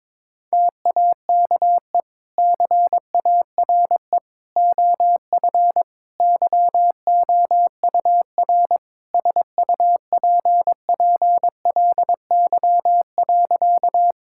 Morse code, heard as a keyed tone, about 22 wpm, 700 Hz. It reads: TAKE CARE OF YOUR SUPPLY.
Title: Sentences from Top 1000 Words - Review Part 5 - 22wpm